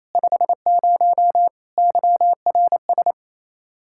Morse code, heard as a keyed tone, about 28 wpm, 700 Hz. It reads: 50 YRS
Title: K1USN SST QSO Practice 28wpm